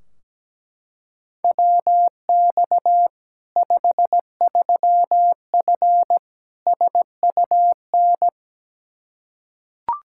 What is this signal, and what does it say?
Morse code, heard as a keyed tone, about 17 wpm, 700 Hz. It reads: WX 53F SUN
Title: Practice QSO - Exchange 2 - Info - 17wpm